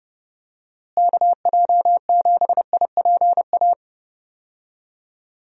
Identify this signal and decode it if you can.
Morse code, heard as a keyed tone, about 30 wpm, 700 Hz. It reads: KJ7IPA
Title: USA Call Signs - 6 Characters - Repeated 2x - 30wpm